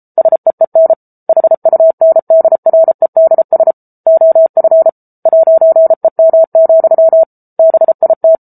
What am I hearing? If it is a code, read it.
Morse code, heard as a keyed tone, about 34 wpm, 650 Hz. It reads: SEEN HUNDREDS OF 'EM, BIT
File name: Book Two / 07-Chapter 7 - 33 WPM